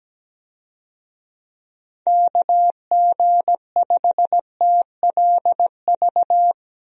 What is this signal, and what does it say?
Morse code, heard as a keyed tone, about 17 wpm, 700 Hz. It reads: KG5TLV
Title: USA Call Signs - 6 Characters - Repeated 3x - 17wpm